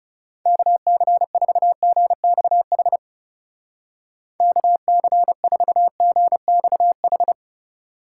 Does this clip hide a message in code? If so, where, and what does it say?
Yes: Morse code from 0.5 to 7.3 s, heard as a keyed tone, about 38 wpm, 700 Hz. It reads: KC4GXH KC4GXH